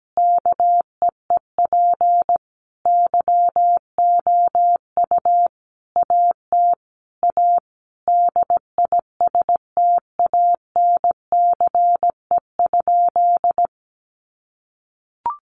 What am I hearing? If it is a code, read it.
Morse code, heard as a keyed tone, about 17 wpm, 700 Hz. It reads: KEEP YOU AT A DISTANCE?
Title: Sentences from Top 1200 Words 17wpm